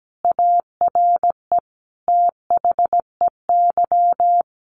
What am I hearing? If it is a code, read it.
Morse code, heard as a keyed tone, about 17 wpm, 700 Hz. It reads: ARE THEY
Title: Sentences from Top 800 Words 17wpm